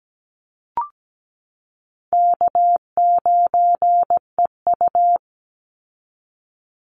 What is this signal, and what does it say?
Morse code, heard as a keyed tone, about 17 wpm, 700 Hz. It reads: K9EU
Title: Easy Call Signs Repeated 2x - 17wpm